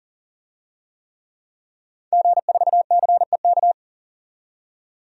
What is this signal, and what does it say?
Morse code, heard as a keyed tone, about 40 wpm, 700 Hz. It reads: G4CEK
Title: DXCC-100 Call Signs - 5 Characters - 40wpm